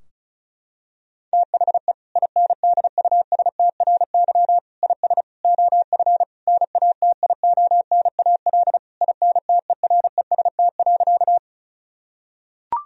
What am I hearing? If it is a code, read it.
Morse code, heard as a keyed tone, about 34 wpm, 700 Hz. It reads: THE INDUSTRY IS OF NATIONAL INTEREST.
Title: Sentences from Top 1000 Words - Review Part 4 - 35wpm